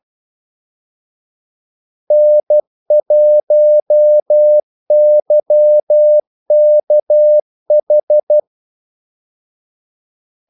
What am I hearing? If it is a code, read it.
Morse code, heard as a keyed tone, about 12 wpm, 600 Hz. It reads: N1YKH
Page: Kaldesignaler 12 wpm